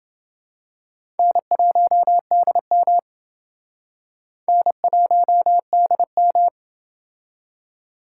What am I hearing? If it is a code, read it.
Morse code, heard as a keyed tone, about 28 wpm, 700 Hz. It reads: N1DM N1DM